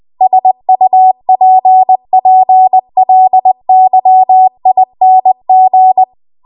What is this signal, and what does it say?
Morse code, heard as a keyed tone, about 20 wpm, 750 Hz. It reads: SUPPLYING
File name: Morse / 20WPM